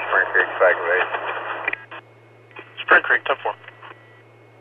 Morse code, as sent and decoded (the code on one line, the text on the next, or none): none